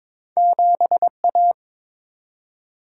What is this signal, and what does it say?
Morse code, heard as a keyed tone, about 22 wpm, 700 Hz. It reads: 7A
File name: practice / 22wpm / Two Letter-Number Combinations - 22wpm